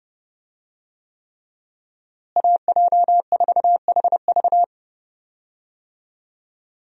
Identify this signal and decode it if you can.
Morse code, heard as a keyed tone, about 30 wpm, 700 Hz. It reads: AJ4HV
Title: USA Call Signs - 5 Characters - Repeated 3x - 30wpm